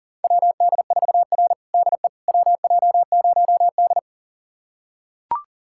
Morse code, heard as a keyed tone, about 40 wpm, 700 Hz. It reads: WD4R DE WJ0D
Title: USA Callsign DE Callsign - 4 Characters - 40wpm